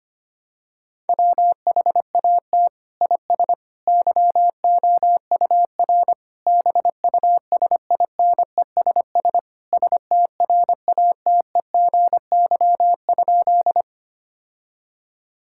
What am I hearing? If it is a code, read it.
Morse code, heard as a keyed tone, about 25 wpm, 700 Hz. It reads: WHAT IS YOUR BUSINESS STRATEGY?
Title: Sentences from Top 1000 Words - Review Part 5 - 25wpm